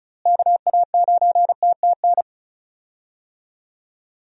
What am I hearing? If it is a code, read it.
Morse code, heard as a keyed tone, about 36 wpm, 700 Hz. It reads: KA9TTN
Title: USA Call Signs - 6 Characters - Repeated 2x - 35wpm